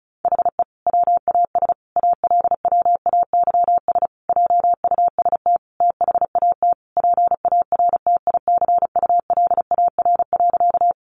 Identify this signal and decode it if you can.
Morse code, heard as a keyed tone, about 34 wpm, 700 Hz. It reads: HE WAS ALWAYS JUST THAT PARTICULAR.
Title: Sentences from Top 600 Words 35wpm